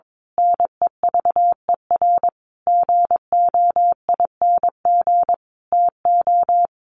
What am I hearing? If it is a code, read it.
Morse code, heard as a keyed tone, about 22 wpm, 700 Hz. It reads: NEVER GOING TO
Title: A Sound Of Thunder 22wpm